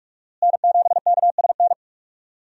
Morse code, heard as a keyed tone, about 45 wpm, 700 Hz. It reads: N7KSN
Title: USA Call Signs - 5 Characters - Repeated 2x - 45wpm